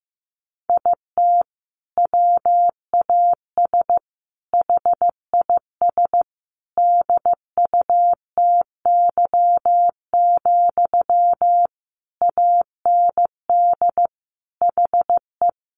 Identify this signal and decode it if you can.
Morse code, heard as a keyed tone, about 15 wpm, 700 Hz. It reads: IT WAS HIS DUTY, AND HE
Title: Sentences from Top 1500 Words 15wpm